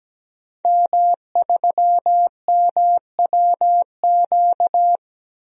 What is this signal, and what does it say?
Morse code, heard as a keyed tone, about 17 wpm, 700 Hz. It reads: M3MWQ
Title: DXCC-100 Call Signs - 5 Characters - 17wpm